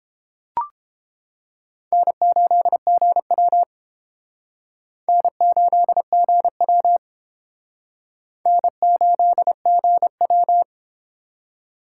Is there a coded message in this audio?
Yes: Morse code from 1.9 to 10.6 s, heard as a keyed tone, about 30 wpm, 700 Hz. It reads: N8GW N8GW N8GW